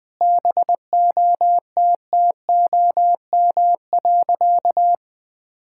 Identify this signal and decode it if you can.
Morse code, heard as a keyed tone, about 21 wpm, 700 Hz. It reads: BOTTOM.
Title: Sentences from Top 1000 Words - Review Part 6 - 20wpm